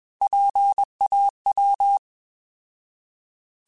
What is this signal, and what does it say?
Morse code, heard as a keyed tone, about 21 wpm, 800 Hz. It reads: PAW
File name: Alphabetical / Alphabetical mp3s / Ps @20